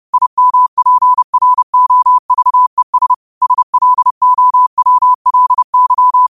Morse code, heard as a keyed tone, about 30 wpm, 1000 Hz. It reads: IMPROVES SLOWLY
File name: qotdfiles / 30wpm